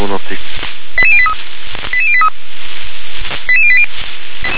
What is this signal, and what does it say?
Five-tone selective call (ZVEI), heard as quick runs of five tones, tones of about 70 ms, with speech. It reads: 88083 88083 88080